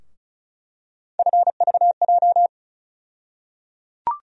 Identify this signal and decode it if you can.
Morse code, heard as a keyed tone, about 34 wpm, 700 Hz. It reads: FVJ
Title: USA Call Sign Postfixes - 3 Letters 35wpm